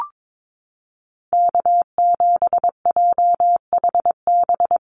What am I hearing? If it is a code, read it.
Morse code, heard as a keyed tone, about 22 wpm, 700 Hz. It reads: K7JHB
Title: USA Call Signs Repeated 2x 22wpm